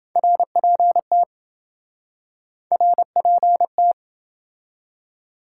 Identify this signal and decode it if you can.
Morse code, heard as a keyed tone, about 26 wpm, 700 Hz. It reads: RPT RPT